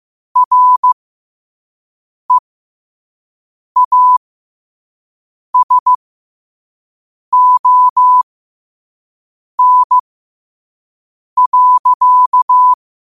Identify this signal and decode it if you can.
Morse code, heard as a keyed tone, about 15 wpm, spaced as for 5 wpm, 1000 Hz. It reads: REASON.
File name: qotdfiles / 5wpm